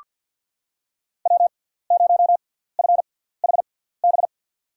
Morse code, heard as a keyed tone, about 48 wpm, spaced as for 17 wpm, 700 Hz. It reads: W0FHB
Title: USA Call Signs - 5 Characters - 6x Character Spacing - 50wpm